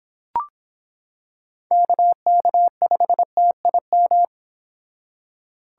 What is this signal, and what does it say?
Morse code, heard as a keyed tone, about 25 wpm, 700 Hz. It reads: KK5TIM